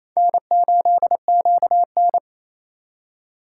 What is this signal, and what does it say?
Morse code, heard as a keyed tone, about 32 wpm, 700 Hz. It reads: N8QN
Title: Speed Racing - USA Call Signs 28wpm